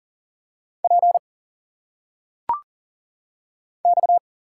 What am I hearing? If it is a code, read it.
Morse code, heard as a keyed tone, about 40 wpm, 700 Hz. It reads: P X
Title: Single Letters - Phonetic Pronunciation 40wpm